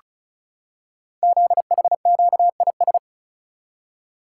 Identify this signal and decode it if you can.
Morse code, heard as a keyed tone, about 36 wpm, 700 Hz. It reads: ZHQIS
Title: Code Groups - 5 Letters - 35wpm